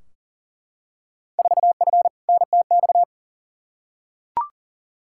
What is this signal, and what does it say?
Morse code, heard as a keyed tone, about 40 wpm, 700 Hz. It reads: 4F NTX
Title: Field Day Exchanges Encore - 40wpm